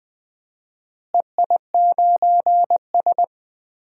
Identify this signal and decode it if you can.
Morse code, heard as a keyed tone, about 20 wpm, 700 Hz. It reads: EI9S
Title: DXCC-100 Call Signs - 4 Characters - Repeated 3x - 20wpm